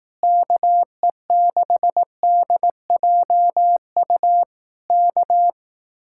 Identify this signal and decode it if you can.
Morse code, heard as a keyed tone, about 18 wpm, 700 Hz. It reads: KE6DJU K
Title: Speed-Racing Common QSO Phrases Encore - 15wpm